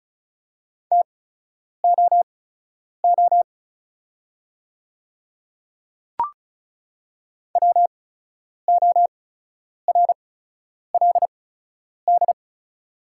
Morse code, heard as a keyed tone, about 36 wpm, spaced as for 10 wpm, 700 Hz. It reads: TOO WORLD
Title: 8x Character Spacing - Top 200 Words - 35wpm